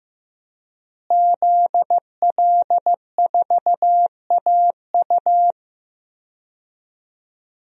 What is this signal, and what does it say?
Morse code, heard as a keyed tone, about 15 wpm, 700 Hz. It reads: ZL4AU
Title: DXCC-100 Call Signs - 5 Characters - Repeated 2x - 15wpm